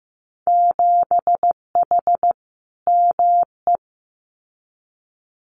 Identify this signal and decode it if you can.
Morse code, heard as a keyed tone, about 15 wpm, 700 Hz. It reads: 7H ME
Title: Winter Field Day Exchanges Encore 15wpm